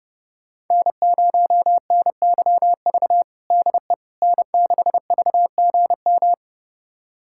Morse code, heard as a keyed tone, about 30 wpm, 700 Hz. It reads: N0NYV DE N6VGM
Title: USA Callsign DE Callsign - 5 Characters - 30wpm